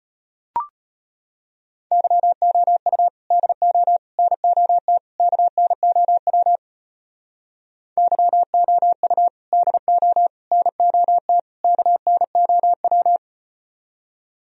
Morse code, heard as a keyed tone, about 38 wpm, 700 Hz. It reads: YOU DO NOT KNOW YOU DO NOT KNOW